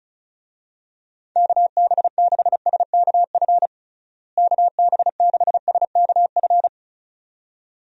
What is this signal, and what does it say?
Morse code, heard as a keyed tone, about 34 wpm, 700 Hz. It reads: KB6SKF KB6SKF
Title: USA Call Signs - 6 Characters - Repeated 2x - 35wpm